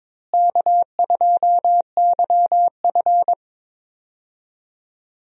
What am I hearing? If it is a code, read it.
Morse code, heard as a keyed tone, about 22 wpm, 700 Hz. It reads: K2YF
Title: USA Call Signs - 4 Characters - Repeated 2x - 22wpm